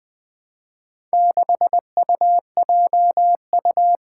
Morse code, heard as a keyed tone, about 20 wpm, 700 Hz. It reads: 6UJU
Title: Four Challenging Letter-Number Combinations 20wpm